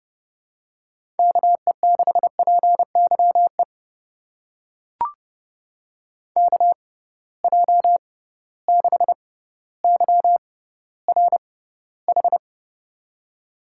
Morse code, heard as a keyed tone, about 30 wpm, spaced as for 15 wpm, 700 Hz. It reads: KE6PYE KJ6YRH
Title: USA Call Signs - 6 Characters - 6x Character Spacing - 30wpm